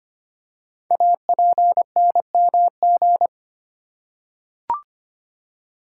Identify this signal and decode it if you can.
Morse code, heard as a keyed tone, about 25 wpm, 690 Hz. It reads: APNMG
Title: Code Groups - 5 Letters - 25wpm